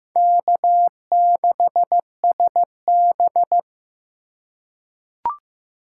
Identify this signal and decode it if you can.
Morse code, heard as a keyed tone, about 15 wpm, 700 Hz. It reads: K6SB
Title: USA Call Signs - 3x Character Spacing - 15wpm